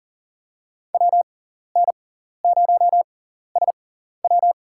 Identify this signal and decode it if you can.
Morse code, heard as a keyed tone, about 37 wpm, spaced as for 13 wpm, 700 Hz. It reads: WN0SW
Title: USA Call Signs - 6x Character Spacing - 40wpm